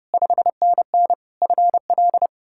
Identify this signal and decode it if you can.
Morse code, heard as a keyed tone, about 40 wpm, 700 Hz. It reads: 5NN FL